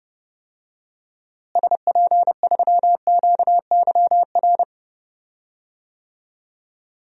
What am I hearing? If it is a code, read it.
Morse code, heard as a keyed tone, about 30 wpm, 700 Hz. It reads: SP3QYR
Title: DXCC-100 Call Signs - 6 Characters - Repeated 3x - 30wpm